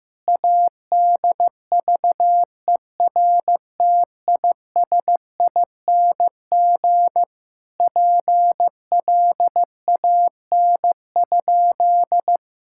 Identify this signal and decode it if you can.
Morse code, heard as a keyed tone, about 15 wpm, 700 Hz. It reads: ADVERTISING PLAN?
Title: Sentences from Top 1000 Words - Review Part 6 - 15wpm